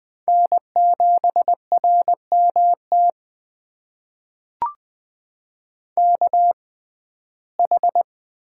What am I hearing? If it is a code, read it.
Morse code, heard as a keyed tone, about 20 wpm, spaced as for 14 wpm, 700 Hz. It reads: N7RMT KH
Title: USA Call Signs - 6x Character Spacing - 20wpm